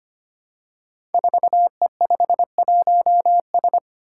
Morse code, heard as a keyed tone, about 25 wpm, 700 Hz. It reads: 4E51S
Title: Five Challenging Letter-Number Combinations 25wpm